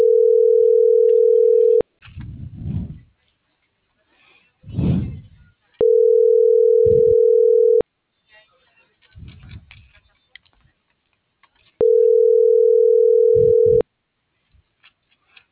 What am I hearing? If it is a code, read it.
Ringback tone (North America), 440 + 480 Hz.